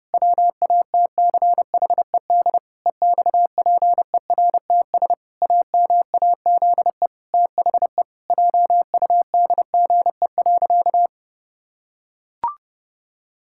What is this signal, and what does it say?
Morse code, heard as a keyed tone, about 30 wpm, 700 Hz. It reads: WATCHED EXPERTS AMAZE THE JUDGE.
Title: Pangram Challenge v2 - 30wpm